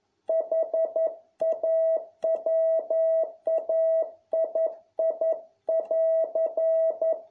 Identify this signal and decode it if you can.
Morse code, heard as a keyed tone, about 11 wpm, 650 Hz. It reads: HAWAII<AR>